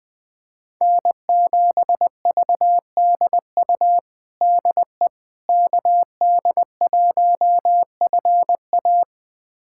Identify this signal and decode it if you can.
Morse code, heard as a keyed tone, about 20 wpm, 700 Hz. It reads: N7VDU DE KD1FA
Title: USA Callsign DE Callsign - 5 Characters - 20wpm